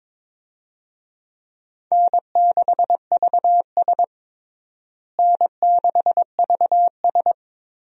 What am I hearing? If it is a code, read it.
Morse code, heard as a keyed tone, about 22 wpm, 700 Hz. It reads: N6VS N6VS
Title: USA Call Signs - 4 Characters - Repeated 3x - 22wpm